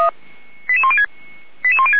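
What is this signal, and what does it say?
Five-tone selective call (ZVEI), heard as quick runs of five tones, tones of about 70 ms. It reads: 88106
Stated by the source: ZVEI five-tone call received by SDR.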